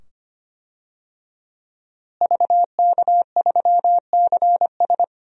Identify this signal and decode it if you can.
Morse code, heard as a keyed tone, about 25 wpm, 700 Hz. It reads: VK3CS
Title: DXCC-100 Call Signs - 5 Characters - Repeated 2x - 25wpm